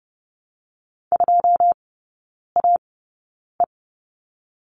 Morse code, heard as a keyed tone, about 30 wpm, spaced as for 9 wpm, 690 Hz. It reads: 2AE
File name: Lesson 17e - Letter u - Three Characters 30wpm